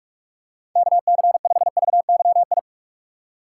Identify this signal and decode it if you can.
Morse code, heard as a keyed tone, about 44 wpm, 700 Hz. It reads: KC5VYI
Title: USA Call Signs - 6 Characters - 4x Character Spacing - 45wpm